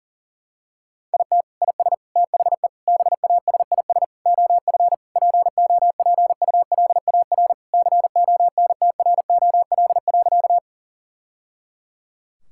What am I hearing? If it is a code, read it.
Morse code, heard as a keyed tone, about 40 wpm, 700 Hz. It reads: IT IS THE BASIS OF POPULAR CONTROL.
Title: Sentences from Top 1000 Words - Review Part 5 - 40wpm